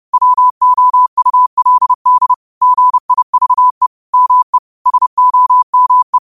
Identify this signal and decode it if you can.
Morse code, heard as a keyed tone, about 30 wpm, 1000 Hz. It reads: WOULD GIVE ME SOME
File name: qotdfiles / 30wpm